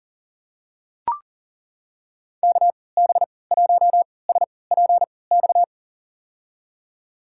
Morse code, heard as a keyed tone, about 40 wpm, spaced as for 23 wpm, 700 Hz. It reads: KB1SPX
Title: USA Call Signs - 6 Characters - 3x Character Spacing - 40wpm